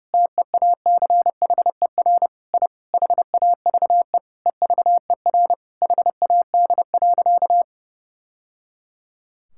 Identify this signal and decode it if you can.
Morse code, heard as a keyed tone, about 30 wpm, 700 Hz. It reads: TEACHER I HAVE EVER HAD.
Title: Sentences from Top 1000 Words - Review Part 6 - 30wpm